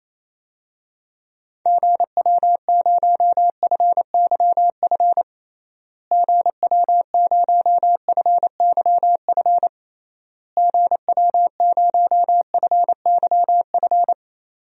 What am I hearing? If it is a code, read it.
Morse code, heard as a keyed tone, about 28 wpm, 700 Hz. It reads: GW0FYF GW0FYF GW0FYF
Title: DXCC-100 Call Signs - 6 Characters - Repeated 3x - 28wpm